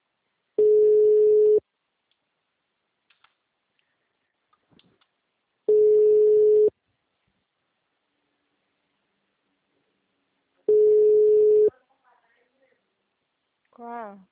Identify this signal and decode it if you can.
Ringback tone (Europe, 425 Hz), with speech.